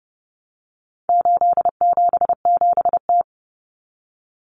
Morse code, heard as a keyed tone, about 30 wpm, 700 Hz. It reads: 877T
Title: Common Cut Numbers - 4-Digit 30wpm